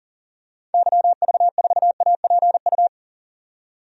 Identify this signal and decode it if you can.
Morse code, heard as a keyed tone, about 40 wpm, 700 Hz. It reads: YV4APU
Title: DXCC-100 Call Signs - 6 Characters - 40wpm